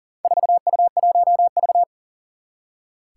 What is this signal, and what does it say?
Morse code, heard as a keyed tone, about 42 wpm, 700 Hz. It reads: 4U1V
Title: Four Challenging Letter-Number Combinations 40wpm